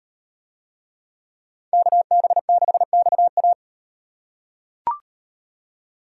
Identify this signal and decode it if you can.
Morse code, heard as a keyed tone, about 35 wpm, 700 Hz. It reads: KB6XA